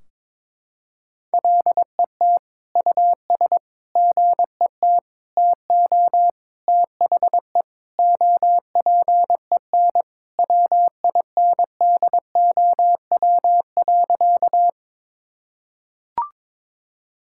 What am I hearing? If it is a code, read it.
Morse code, heard as a keyed tone, about 22 wpm, 700 Hz. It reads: LET US GET TO THE OPEN WINDOW.
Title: Sentences from Top 1600 Words 22wpm